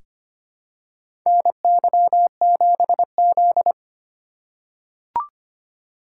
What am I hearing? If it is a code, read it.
Morse code, heard as a keyed tone, about 25 wpm, 700 Hz. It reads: NY7Z
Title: K1USN SST QSO Practice - 25wpm